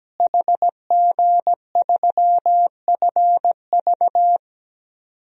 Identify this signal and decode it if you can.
Morse code, heard as a keyed tone, about 17 wpm, 700 Hz. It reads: HG3FV
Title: DXCC-100 Call Signs - 5 Characters - Repeated 3x - 17wpm